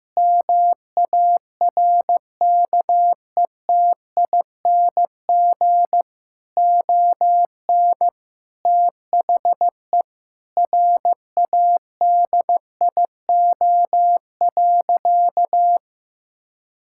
Morse code, heard as a keyed tone, about 15 wpm, 700 Hz. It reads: MARKETING ON THE RADIO.
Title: Sentences from Top 1000 Words - Review Part 5 - 15wpm